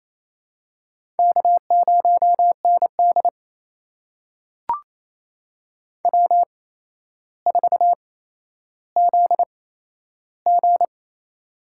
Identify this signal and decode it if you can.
Morse code, heard as a keyed tone, about 28 wpm, spaced as for 12 wpm, 700 Hz. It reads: K0ND W4ZG
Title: USA Call Signs - 4 Characters - 8x Character Spacing - 28wpm